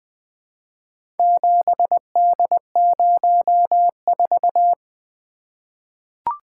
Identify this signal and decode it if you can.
Morse code, heard as a keyed tone, about 20 wpm, 700 Hz. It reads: 7D04
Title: Four Letter-Number Combinations - 20wpm